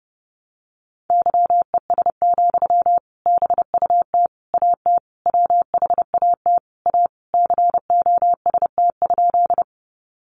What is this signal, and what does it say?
Morse code, heard as a keyed tone, about 30 wpm, 700 Hz. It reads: YES, BUT AT WHAT A COST?
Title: Sentences from Top 800 Words 30wpm